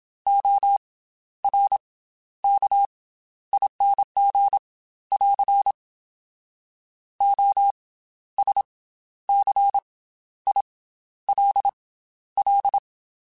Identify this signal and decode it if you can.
Morse code, heard as a keyed tone, about 27 wpm, spaced as for 12 wpm, 800 Hz. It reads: ORKING <AR> OSCILL
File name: Special INGs / Special INGs-mp3s / Slow ing @25